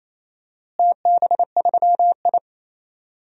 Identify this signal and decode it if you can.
Morse code, heard as a keyed tone, about 28 wpm, 700 Hz. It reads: TB3I
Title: DXCC-100 Call Signs - 4 Characters - Repeated 3x - 28wpm